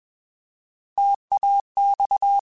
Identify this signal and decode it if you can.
Morse code, heard as a keyed tone, about 21 wpm, 800 Hz. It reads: TAX
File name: Alphabetical / Alphabetical mp3s / Ts @20